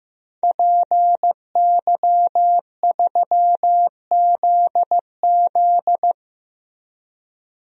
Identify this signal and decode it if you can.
Morse code, heard as a keyed tone, about 16 wpm, 700 Hz. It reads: PY3ZZ
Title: DXCC-100 Call Signs - 5 Characters - Repeated 2x - 15wpm